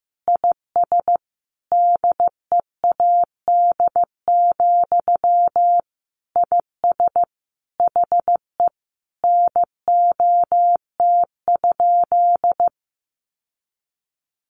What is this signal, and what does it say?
Morse code, heard as a keyed tone, about 15 wpm, 700 Hz. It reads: IS DEAD, IS HE NOT?
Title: Sentences from Top 1500 Words 15wpm